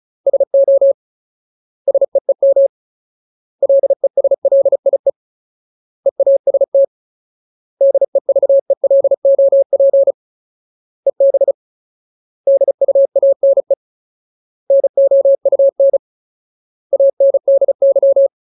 Morse code, heard as a keyed tone, about 34 wpm, 550 Hz. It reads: SO SEEM LESLIE EAST DEVELOP EB DUANE NOUN ANDY
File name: Ultimate Word List / 35WPM Ultimate